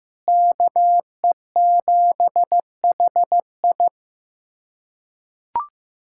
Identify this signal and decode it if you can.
Morse code, heard as a keyed tone, about 15 wpm, 700 Hz. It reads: KE7HI
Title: USA Callsign DE Callsign - 5 Characters - 15wpm